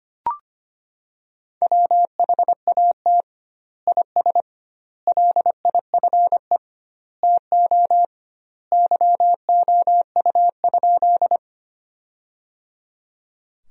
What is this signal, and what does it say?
Morse code, heard as a keyed tone, about 25 wpm, 700 Hz. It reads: WHAT IS LIFE TO YOU?
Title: 2x Word-Spacing - Sentences from Top 300 Words - 25wpm